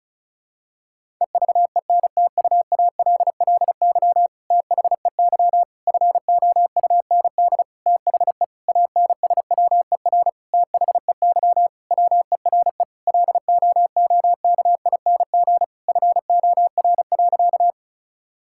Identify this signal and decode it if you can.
Morse code, heard as a keyed tone, about 36 wpm, 700 Hz. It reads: EVENTUALLY THEY FOUND THE ANSWER THEY WERE LOOKING FOR.
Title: Sentences from Top 1000 Words - Review Part 6 - 35wpm